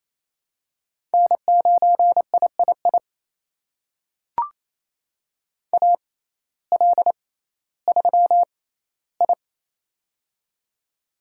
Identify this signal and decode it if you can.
Morse code, heard as a keyed tone, about 27 wpm, spaced as for 15 wpm, 700 Hz. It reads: N9III AL3I
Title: USA Call Signs - 6x Character Spacing - 28wpm